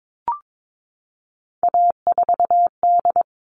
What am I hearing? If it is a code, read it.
Morse code, heard as a keyed tone, about 22 wpm, 700 Hz. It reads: A4D
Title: All Cut Numbers - 3-Digit 22wpm